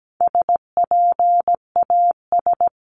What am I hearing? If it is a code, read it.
Morse code, heard as a keyed tone, about 17 wpm, 700 Hz. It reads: SPAS
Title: CWT Exchange 17wpm